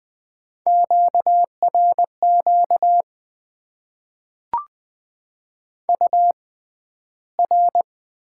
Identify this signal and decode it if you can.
Morse code, heard as a keyed tone, about 20 wpm, spaced as for 12 wpm, 700 Hz. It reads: QRQ UR
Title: Top 90 QSO Elements - 6x Character Spacing - 20wpm